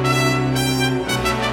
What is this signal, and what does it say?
There is no code.